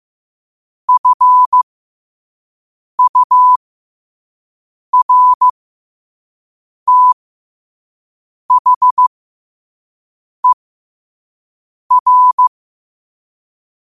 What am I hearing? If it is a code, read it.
Morse code, heard as a keyed tone, about 15 wpm, spaced as for 5 wpm, 1000 Hz. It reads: FURTHER
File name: qotdfiles / 5wpm